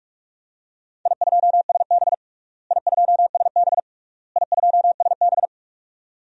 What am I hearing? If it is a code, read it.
Morse code, heard as a keyed tone, about 46 wpm, 690 Hz. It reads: I2SB I2SB I2SB
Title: DXCC-100 Call Signs - 4 Characters - Repeated 3x - 45wpm